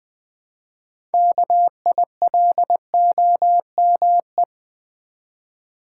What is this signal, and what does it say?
Morse code, heard as a keyed tone, about 20 wpm, 700 Hz. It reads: KILOME
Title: Build A Word - 3 Letters At A Time - 9-Letter Words - 20wpm